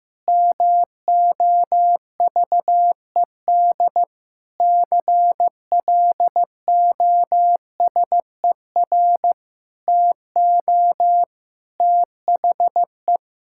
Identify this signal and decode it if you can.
Morse code, heard as a keyed tone, about 15 wpm, 700 Hz. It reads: MOVED CLOSER TO THE
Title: Pangram Challenge v2 - 15wpm